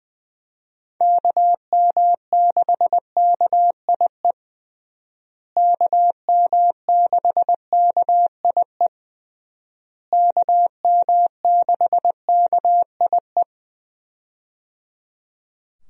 Morse code, heard as a keyed tone, about 20 wpm, 700 Hz. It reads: KM6KIE KM6KIE KM6KIE
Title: USA Call Signs - 6 Characters - Repeated 3x - 20wpm